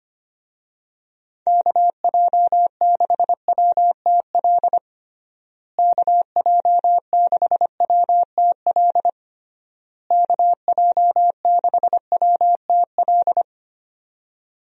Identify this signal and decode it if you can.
Morse code, heard as a keyed tone, about 25 wpm, 700 Hz. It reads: KJ6WTL KJ6WTL KJ6WTL
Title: USA Call Signs - 6 Characters - Repeated 3x - 25wpm